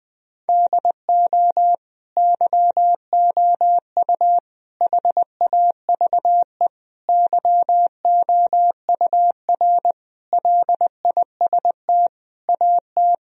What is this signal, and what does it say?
Morse code, heard as a keyed tone, about 20 wpm, 700 Hz. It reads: DO YOU HAVE YOUR LIST AT
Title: Sentences from Top 1000 Words - Review Part 2 - 20wpm